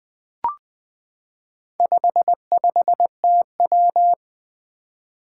Morse code, heard as a keyed tone, about 20 wpm, 700 Hz. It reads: 55TW
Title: Four Challenging Letter-Number Combinations 20wpm